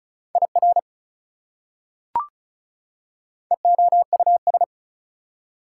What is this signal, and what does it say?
Morse code, heard as a keyed tone, about 36 wpm, 700 Hz. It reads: IR EOUS
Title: Prefixes and Suffixes 35wpm